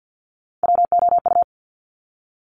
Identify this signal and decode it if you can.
Morse code, heard as a keyed tone, about 50 wpm, 700 Hz. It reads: WOU
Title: Top Three Letter Combinations in Words 50wpm